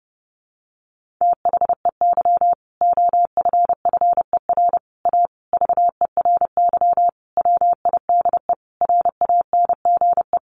Morse code, heard as a keyed tone, about 28 wpm, 700 Hz. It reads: THEY OFFER A VERY WIDE RANGE
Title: Sentences from Top 900 Words 30wpm